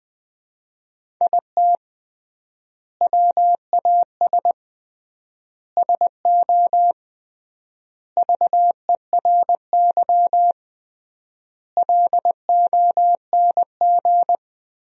Morse code, heard as a keyed tone, about 20 wpm, 700 Hz. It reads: IT WAS SO VERY LONG
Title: 3x Word-Spacing - Sentences from Top 500 Words - 20wpm